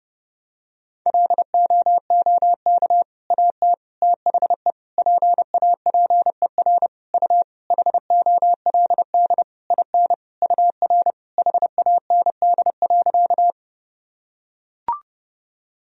Morse code, heard as a keyed tone, about 30 wpm, 700 Hz. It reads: LOOK AT THE PAPER U HOLD IN UR HAND.
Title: Sentences from Top 400 Words and Abbreviations 30wpm